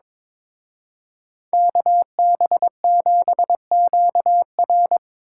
Morse code, heard as a keyed tone, about 25 wpm, 700 Hz. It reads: KB7QR
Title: Speed Racing - USA 2x2 Call Signs 22wpm